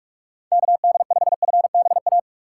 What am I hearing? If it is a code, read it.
Morse code, heard as a keyed tone, about 46 wpm, 700 Hz. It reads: KD5FBA
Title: K1USN SST QSO Practice - 45wpm